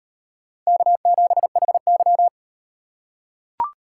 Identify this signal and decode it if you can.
Morse code, heard as a keyed tone, about 38 wpm, 700 Hz. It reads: K7HY